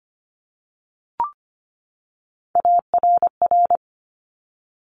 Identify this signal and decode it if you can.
Morse code, heard as a keyed tone, about 25 wpm, 700 Hz. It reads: ARR
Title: Top Three Letter Combinations in Words 25wpm